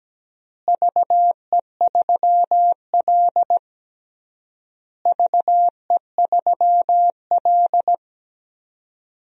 Morse code, heard as a keyed tone, about 17 wpm, 700 Hz. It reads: VE3L VE3L
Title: DXCC-100 Call Signs - 4 Characters - Repeated 3x - 17wpm